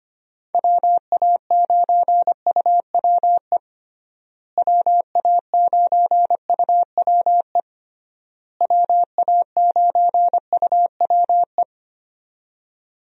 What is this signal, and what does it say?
Morse code, heard as a keyed tone, about 25 wpm, 700 Hz. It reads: WA9UWE WA9UWE WA9UWE
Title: USA Call Signs - 6 Characters - Repeated 3x - 25wpm